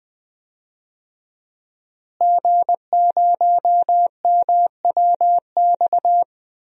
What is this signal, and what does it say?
Morse code, heard as a keyed tone, about 20 wpm, 700 Hz. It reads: G0MWX
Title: DXCC-100 Call Signs - 5 Characters - Repeated 2x - 20wpm